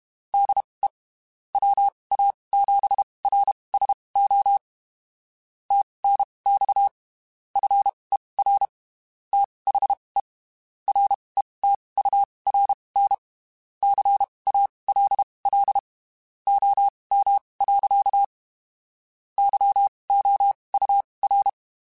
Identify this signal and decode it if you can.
Morse code, heard as a keyed tone, about 32 wpm, spaced as for 23 wpm, 800 Hz. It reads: DE WA7RSO TNX FER THE RETURN CALL OM. YOUR
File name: (spaced) / QSOs Spaced-mp3s / kc7haa @30 sp